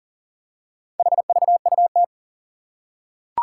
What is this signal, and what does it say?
Morse code, heard as a keyed tone, about 40 wpm, 700 Hz. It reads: HVUT
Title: Four Challenging Letter-Number Combinations 40wpm